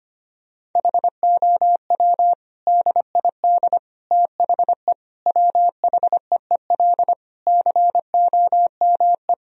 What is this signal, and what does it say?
Morse code, heard as a keyed tone, about 25 wpm, 700 Hz. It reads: HOW DID THE WHEEL COME
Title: Sentences from Top 1000 Words - Review Part 3 - 25wpm